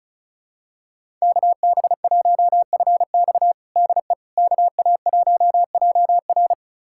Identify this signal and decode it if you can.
Morse code, heard as a keyed tone, about 36 wpm, 700 Hz. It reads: KB1FX DE KA1JR
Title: USA Callsign DE Callsign - 5 Characters - 35wpm